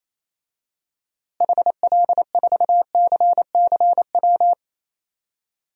Morse code, heard as a keyed tone, about 28 wpm, 700 Hz. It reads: HL4CCW
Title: DXCC-100 Call Signs - 6 Characters - Repeated 3x - 28wpm